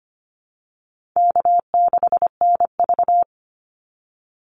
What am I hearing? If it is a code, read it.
Morse code, heard as a keyed tone, about 25 wpm, 700 Hz. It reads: K6NV
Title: Easy Call Signs - 25wpm